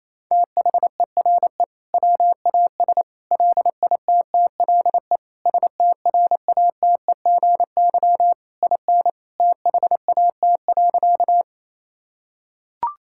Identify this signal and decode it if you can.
Morse code, heard as a keyed tone, about 28 wpm, 700 Hz. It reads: THERE WAS LITTLE STRATEGY IN THAT.
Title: Sentences from Top 1000 Words - Review Part 5 - 28wpm